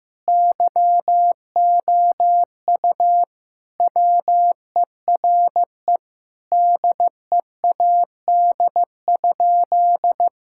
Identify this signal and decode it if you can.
Morse code, heard as a keyed tone, about 16 wpm, 700 Hz. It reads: YOU WERE DEAD?
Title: Sentences from Top 1600 Words 15wpm